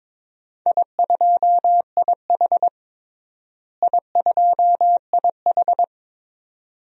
Morse code, heard as a keyed tone, about 22 wpm, 700 Hz. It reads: I2IH I2IH
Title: DXCC-100 Call Signs - 4 Characters - Repeated 2x - 22wpm